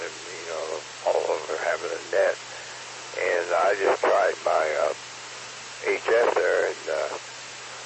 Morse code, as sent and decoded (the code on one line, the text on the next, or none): none